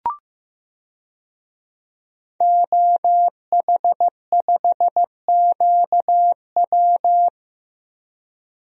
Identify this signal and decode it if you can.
Morse code, heard as a keyed tone, about 15 wpm, 700 Hz. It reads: OH5QW
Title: DXCC-100 Call Signs - 5 Characters - Repeated 2x - 15wpm